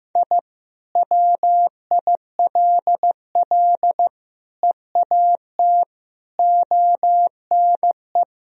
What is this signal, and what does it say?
Morse code, heard as a keyed tone, about 15 wpm, 700 Hz. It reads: I WILL EAT ONE
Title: Sentences from Top 1000 Words - Review Part 1 - 15wpm